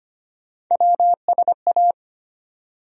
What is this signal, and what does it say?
Morse code, heard as a keyed tone, about 25 wpm, 700 Hz. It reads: WSA
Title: USA Call Sign Postfixes - 3 Letters 25wpm